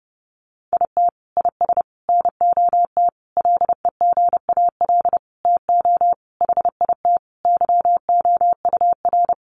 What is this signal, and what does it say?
Morse code, heard as a keyed tone, about 30 wpm, 700 Hz. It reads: IT IS NOT LEGAL TO HIT YOUR
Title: Sentences from Top 1200 Words 30wpm